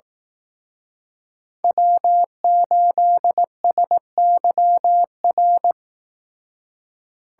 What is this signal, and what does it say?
Morse code, heard as a keyed tone, about 18 wpm, 700 Hz. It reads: W8SYR
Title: Speed Racing - USA 1x3 Call Signs 15wpm